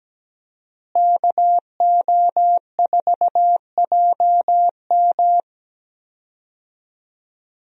Morse code, heard as a keyed tone, about 17 wpm, 700 Hz. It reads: KO4JM
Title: USA Call Signs - 5 Characters - Repeated 2x - 17wpm